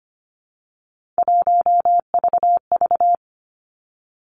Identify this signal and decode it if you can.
Morse code, heard as a keyed tone, about 25 wpm, 700 Hz. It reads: 1VV
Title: Three Challenging Letter-Number Combinations 25wpm